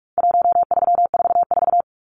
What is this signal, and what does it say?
Morse code, heard as a keyed tone, about 44 wpm, 700 Hz. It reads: 1344
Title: Numbers - 4-Digit 45wpm